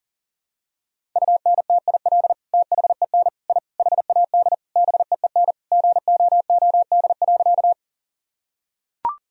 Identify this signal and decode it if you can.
Morse code, heard as a keyed tone, about 37 wpm, 700 Hz. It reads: UNTIL THEN I HAD BEEN GOOD.
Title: Sentences from Top 1000 Words - Review Part 1 - 40wpm